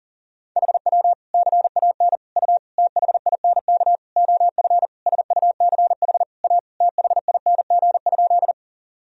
Morse code, heard as a keyed tone, about 40 wpm, 700 Hz. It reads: HW CAN U THINK OF SUCH A THING?
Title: Sentences from Top 200 Words and Abbreviations 40wpm